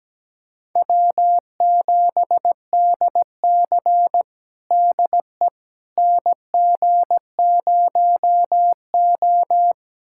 Morse code, heard as a keyed tone, about 17 wpm, 700 Hz. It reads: W7DC DE NG0O
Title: USA Callsign DE Callsign - 4 Characters - 17wpm